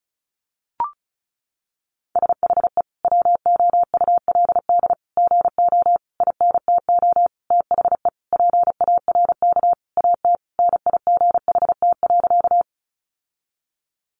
Morse code, heard as a keyed tone, about 34 wpm, 700 Hz. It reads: SHE WOULD GO INTO THE PARK AT NIGHT.
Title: Sentences from Top 1300 Words 35wpm